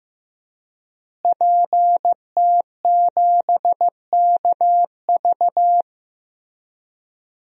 Morse code, heard as a keyed tone, about 15 wpm, 700 Hz. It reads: PT7KV
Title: DXCC-100 Call Signs - 5 Characters - Repeated 3x - 15wpm